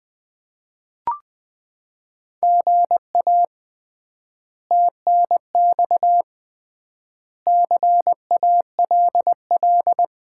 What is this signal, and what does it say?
Morse code, heard as a keyed tone, about 20 wpm, 700 Hz. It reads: GA TNX CALL
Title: Common QSO Phrases - 3x Word-Spacing - 20wpm